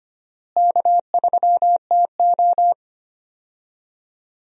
Morse code, heard as a keyed tone, about 25 wpm, 700 Hz. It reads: K3TO
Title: USA Call Signs - 4 Characters - Repeated 3x - 25wpm